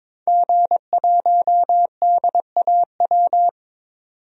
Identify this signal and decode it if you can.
Morse code, heard as a keyed tone, about 21 wpm, 700 Hz. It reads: G1DAW
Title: DXCC-100 Call Signs - 5 Characters - 22wpm